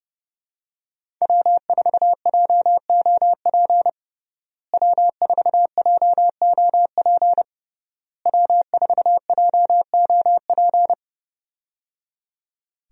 Morse code, heard as a keyed tone, about 30 wpm, 700 Hz. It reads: W4JOP W4JOP W4JOP
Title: USA Call Signs - 5 Characters - Repeated 3x - 30wpm